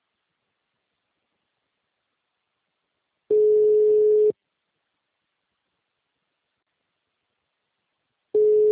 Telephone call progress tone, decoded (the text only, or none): ringback tone